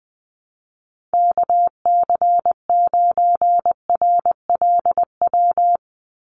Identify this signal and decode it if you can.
Morse code, heard as a keyed tone, about 19 wpm, 700 Hz. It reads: KC9RLW
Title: USA Call Signs 20wpm